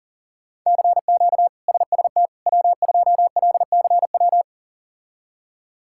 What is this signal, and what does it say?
Morse code, heard as a keyed tone, about 40 wpm, 700 Hz. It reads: CQ SST W2LCW
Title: K1USN SST QSO Practice - 40wpm